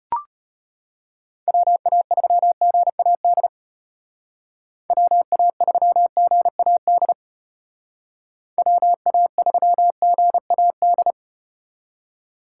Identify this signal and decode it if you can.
Morse code, heard as a keyed tone, about 35 wpm, 700 Hz. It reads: WA3GAD WA3GAD WA3GAD